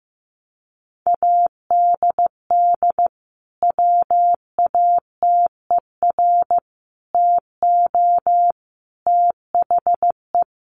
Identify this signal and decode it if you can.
Morse code, heard as a keyed tone, about 15 wpm, 700 Hz. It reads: ADD WATER TO THE
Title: Sentences from Top 1500 Words 15wpm